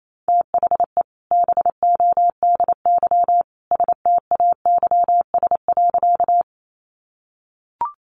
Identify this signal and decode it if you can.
Morse code, heard as a keyed tone, about 28 wpm, 700 Hz. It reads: THE BODY STAYS.
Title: A Sound Of Thunder 28wpm